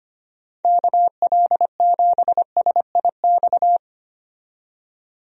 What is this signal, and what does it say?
Morse code, heard as a keyed tone, about 25 wpm, 700 Hz. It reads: KL7SIX
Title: DXCC-100 Call Signs - 6 Characters - 25wpm